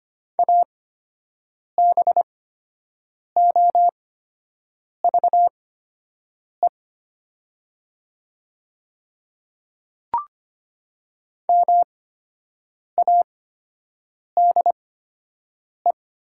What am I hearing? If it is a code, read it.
Morse code, heard as a keyed tone, about 25 wpm, spaced as for 7 wpm, 700 Hz. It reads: ABOVE MADE
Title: 8x Character Spacing - Top 500 Words - 25wpm